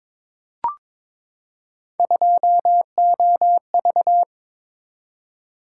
Morse code, heard as a keyed tone, about 22 wpm, 700 Hz. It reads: 2OV
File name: practice / 22wpm / Three Letter-Number Combinations - 22wpm